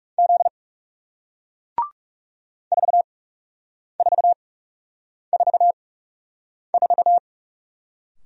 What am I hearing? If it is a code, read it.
Morse code, heard as a keyed tone, about 39 wpm, spaced as for 8 wpm, 700 Hz. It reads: Z 4444